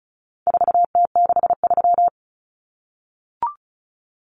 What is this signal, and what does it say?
Morse code, heard as a keyed tone, about 34 wpm, 700 Hz. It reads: 4T63
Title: Common Cut Numbers - 4-Digit 35wpm